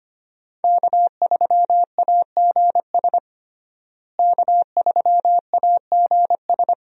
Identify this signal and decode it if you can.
Morse code, heard as a keyed tone, about 25 wpm, 700 Hz. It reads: K3AGS K3AGS
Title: USA Call Signs - 5 Characters - Repeated 3x - 25wpm